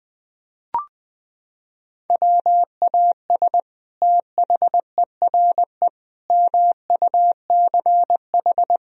Morse code, heard as a keyed tone, about 20 wpm, 700 Hz. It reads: WAS THERE MUCH
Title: Sentences from Top 1000 Words - Review Part 4 - 20wpm